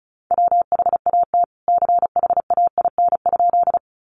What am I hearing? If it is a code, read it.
Morse code, heard as a keyed tone, about 34 wpm, 700 Hz. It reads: WHAT CHAIN?
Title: Sentences from Top 1500 Words 35wpm